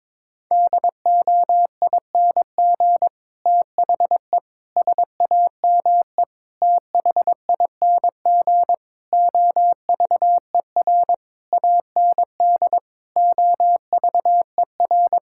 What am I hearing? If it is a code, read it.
Morse code, heard as a keyed tone, about 22 wpm, 700 Hz. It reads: DOING THE SAME THING OVER AND OVER
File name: Sets of 9 Words - 22wpm